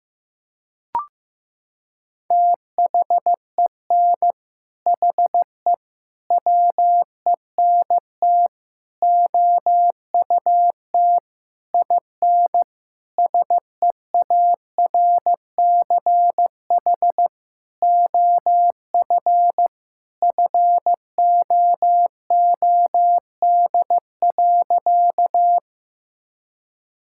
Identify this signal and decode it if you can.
Morse code, heard as a keyed tone, about 15 wpm, 700 Hz. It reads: THEN HE WENT OUT IN SEARCH OF FOOD.
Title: Sentences from Top 1000 Words - Review Part 5 - 15wpm